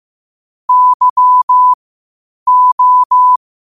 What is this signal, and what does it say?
Morse code, heard as a keyed tone, about 15 wpm, spaced as for 8 wpm, 1000 Hz. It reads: YO
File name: qotdfiles / 8wpm